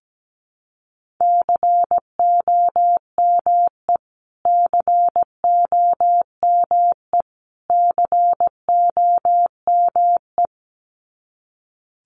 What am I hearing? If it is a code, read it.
Morse code, heard as a keyed tone, about 17 wpm, 700 Hz. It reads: COME COME COME
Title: Top 500 Words Repeated 17wpm